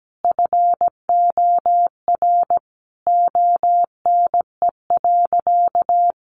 Morse code, heard as a keyed tone, about 17 wpm, 700 Hz. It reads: FOR ONE.
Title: Sentences from Top 300 Words 17wpm